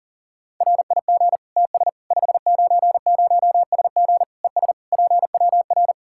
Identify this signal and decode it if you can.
Morse code, heard as a keyed tone, about 40 wpm, 700 Hz. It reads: RIG TS 590SG ES PWR
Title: K1USN SST QSO Practice 40wpm